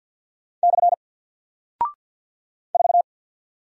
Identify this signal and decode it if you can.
Morse code, heard as a keyed tone, about 48 wpm, 700 Hz. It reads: / 4
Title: Single Character (Letter:Number:Punctuation:Prosign) Alt 50wpm